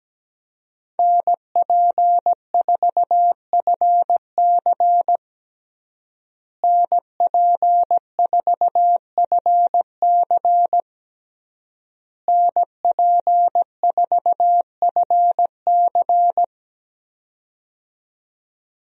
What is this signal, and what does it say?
Morse code, heard as a keyed tone, about 17 wpm, 700 Hz. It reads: NP4FC NP4FC NP4FC
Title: USA Call Signs - 5 Characters - Repeated 3x - 17wpm